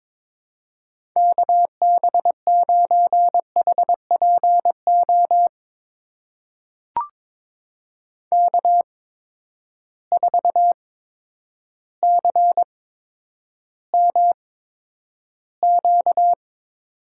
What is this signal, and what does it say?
Morse code, heard as a keyed tone, about 22 wpm, spaced as for 10 wpm, 700 Hz. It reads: KB9HPO K4CMQ
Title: USA Call Signs - 8x Character Spacing - 22wpm